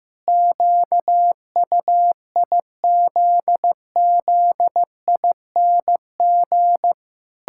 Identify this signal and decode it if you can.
Morse code, heard as a keyed tone, about 15 wpm, 700 Hz. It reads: QUIZZING
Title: Pangram Challenge v2 - 15wpm